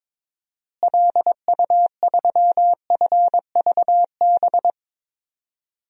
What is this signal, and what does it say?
Morse code, heard as a keyed tone, about 22 wpm, 700 Hz. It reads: LU3FVB
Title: DXCC-100 Call Signs - 6 Characters - Repeated 3x - 22wpm